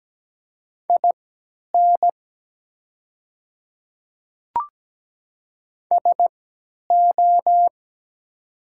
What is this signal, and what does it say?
Morse code, heard as a keyed tone, about 17 wpm, spaced as for 10 wpm, 700 Hz. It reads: IN SO
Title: 3x Character Spacing - Top 100 Words - 17wpm